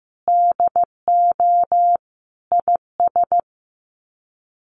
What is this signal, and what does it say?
Morse code, heard as a keyed tone, about 15 wpm, 700 Hz. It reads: DO IS
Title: Sets of 5 Words Encore2 15wpm